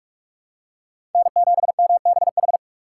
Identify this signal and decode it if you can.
Morse code, heard as a keyed tone, about 46 wpm, 700 Hz. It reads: N7MBH
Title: USA Call Signs - 3x Character Spacing - 45wpm